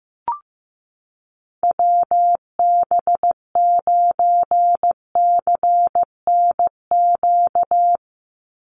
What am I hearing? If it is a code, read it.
Morse code, heard as a keyed tone, about 15 wpm, 700 Hz. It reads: WB9CNQ
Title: USA Call Signs 15wpm